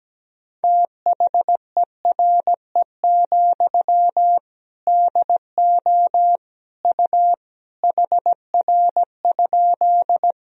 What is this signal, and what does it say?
Morse code, heard as a keyed tone, about 17 wpm, 700 Hz. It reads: THERE, DO U HR?
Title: Sentences from Top 300 Words and Abbreviations 17wpm